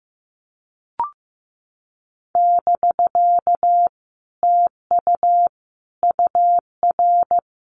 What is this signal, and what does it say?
Morse code, heard as a keyed tone, about 16 wpm, 700 Hz. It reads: <BK> TU UR
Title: Common QSO Phrases Encore 15wpm